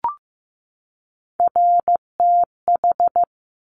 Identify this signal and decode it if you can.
Morse code, heard as a keyed tone, about 15 wpm, 700 Hz. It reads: RTH
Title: Top Three Letter Combinations in Words 15wpm